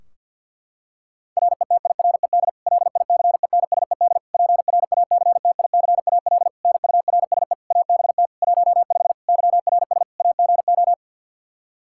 Morse code, heard as a keyed tone, about 51 wpm, 700 Hz. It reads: RETIRED LICENSED PRACTICAL NURSE ABT 15 YRS AGO
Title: K1USN SST QSO Practice 50wpm